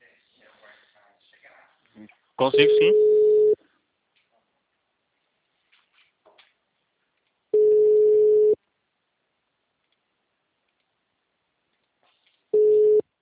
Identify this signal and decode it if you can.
Ringback tone (Europe, 425 Hz), with speech.
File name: out- -2020- - - .6109